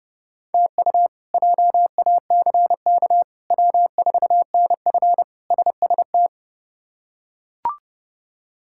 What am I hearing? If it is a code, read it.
Morse code, heard as a keyed tone, about 30 wpm, 700 Hz. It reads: TU JACK W4NF SST
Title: K1USN SST QSO Practice - 30wpm